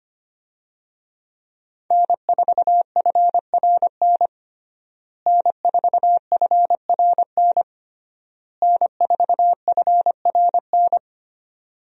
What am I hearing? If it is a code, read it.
Morse code, heard as a keyed tone, about 25 wpm, 700 Hz. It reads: N4FRN N4FRN N4FRN
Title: USA Call Signs - 5 Characters - Repeated 3x - 25wpm